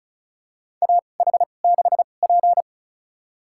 Morse code, heard as a keyed tone, about 36 wpm, spaced as for 25 wpm, 700 Hz. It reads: AH6P
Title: USA Call Signs - 2x Character Spacing - 35wpm